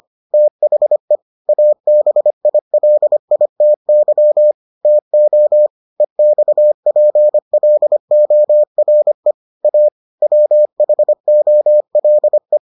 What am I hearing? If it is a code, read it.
Morse code, heard as a keyed tone, about 25 wpm, 600 Hz. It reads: THE ABILITY TO EXPLORE A WHOLE